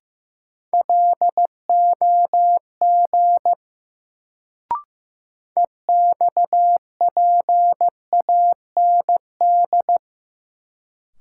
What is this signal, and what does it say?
Morse code, heard as a keyed tone, about 15 wpm, 700 Hz. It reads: LOG EXPAND
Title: Sentences from Top 1400 Words - New Word Drill Alt - 15wpm